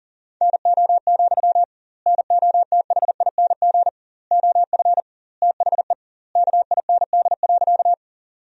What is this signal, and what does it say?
Morse code, heard as a keyed tone, about 37 wpm, 700 Hz. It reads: NO, NOTHING OF THE KIND.
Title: 2x Word-Spacing - Sentences from Top 500 Words - 40wpm